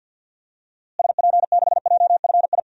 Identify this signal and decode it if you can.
Morse code, heard as a keyed tone, about 48 wpm, 700 Hz. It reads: SP6JFI
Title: DXCC-100 Call Signs - 6 Characters - Repeated 2x - 50wpm